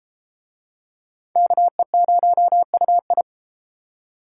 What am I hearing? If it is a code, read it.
Morse code, heard as a keyed tone, about 30 wpm, 700 Hz. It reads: KE0UI